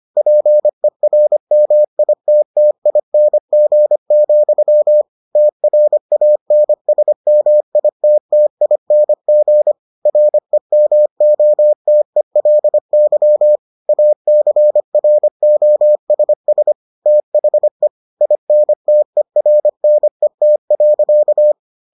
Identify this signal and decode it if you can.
Morse code, heard as a keyed tone, about 25 wpm, 600 Hz. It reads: PERMITTING, TRANSMITTING REMOTELY ACROSS THE INTERNET.